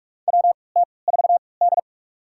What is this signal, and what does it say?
Morse code, heard as a keyed tone, about 45 wpm, spaced as for 25 wpm, 700 Hz. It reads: WT4D
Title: USA Call Signs - 4 Characters - 3x Character Spacing - 45wpm